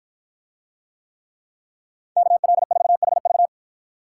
Morse code, heard as a keyed tone, about 49 wpm, 700 Hz. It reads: KL4HV